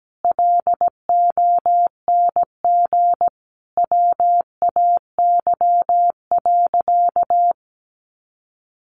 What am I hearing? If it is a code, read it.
Morse code, heard as a keyed tone, about 17 wpm, 700 Hz. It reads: LONG WAY.
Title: Sentences from Top 600 Words 17wpm